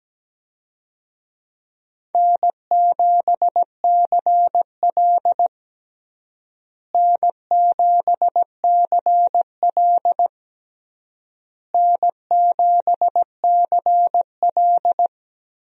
Morse code, heard as a keyed tone, about 17 wpm, 700 Hz. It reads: N7CL N7CL N7CL
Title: USA Call Signs - 4 Characters - Repeated 3x - 17wpm